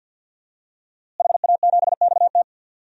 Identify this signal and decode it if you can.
Morse code, heard as a keyed tone, about 50 wpm, 700 Hz. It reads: HA7XT
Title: DXCC-100 Call Signs - 5 Characters - Repeated 2x - 50wpm